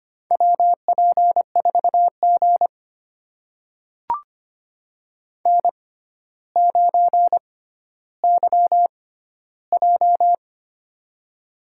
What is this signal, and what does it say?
Morse code, heard as a keyed tone, about 25 wpm, spaced as for 13 wpm, 700 Hz. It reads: WP4G N9YJ
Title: USA Call Signs - 4 Characters - 6x Character Spacing - 25wpm